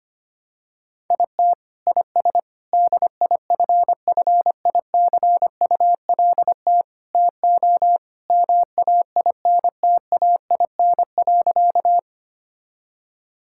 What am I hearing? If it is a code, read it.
Morse code, heard as a keyed tone, about 25 wpm, 700 Hz. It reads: IT IS DIFFICULT TO MAINTAIN.
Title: Sentences from Top 1000 Words - Review Part 5 - 25wpm